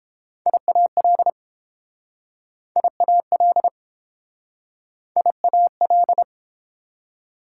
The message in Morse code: IAL IAL IAL

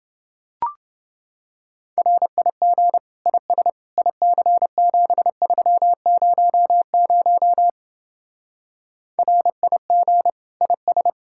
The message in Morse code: RIG IS IC7300 RIG IS

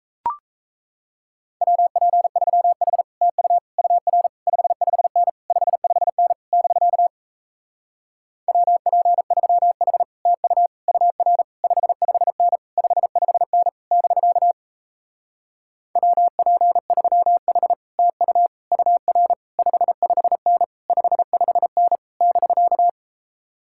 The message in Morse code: WP3H TU UR 55N 55N <BK> WP3H TU UR 55N 55N <BK> WP3H TU UR 55N 55N <BK>

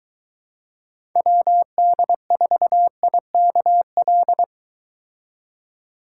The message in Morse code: WD4IKL

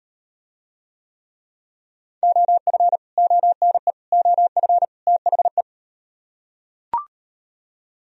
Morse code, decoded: OF ONE OF THE

